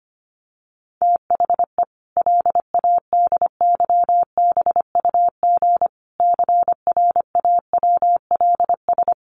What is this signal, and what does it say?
Morse code, heard as a keyed tone, about 25 wpm, 700 Hz. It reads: THE LADYBUG CRAWLS
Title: Additional Easy Phrases 25wpm